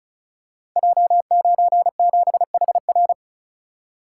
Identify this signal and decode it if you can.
Morse code, heard as a keyed tone, about 36 wpm, 700 Hz. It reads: J97HR
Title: Code Groups - 5 Letter:Numbers - 35wpm